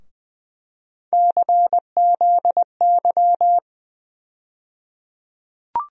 Morse code, heard as a keyed tone, about 20 wpm, 700 Hz. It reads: CZY
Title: USA Call Sign Postfixes - 3 Letters 20wpm